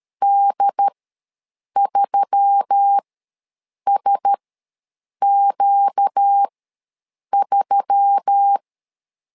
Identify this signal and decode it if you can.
Morse code, heard as a keyed tone, about 13 wpm, spaced as for 7 wpm, 800 Hz. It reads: D3SQ3
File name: LEZIONI / 4 Q1S3 ws 12wpm 2038